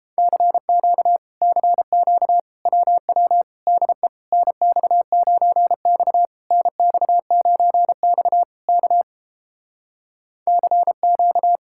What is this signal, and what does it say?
Morse code, heard as a keyed tone, about 29 wpm, 700 Hz. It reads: CQ CQ WW DE NX9X NX9X K CQ